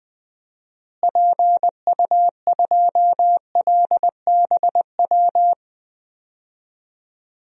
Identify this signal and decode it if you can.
Morse code, heard as a keyed tone, about 21 wpm, 700 Hz. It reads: PU2LBW
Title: DXCC-100 Call Signs - 6 Characters - Repeated 3x - 20wpm